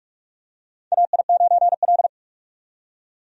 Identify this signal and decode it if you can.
Morse code, heard as a keyed tone, about 46 wpm, 700 Hz. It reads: AI9L
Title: USA Call Signs - 4 Characters - 45wpm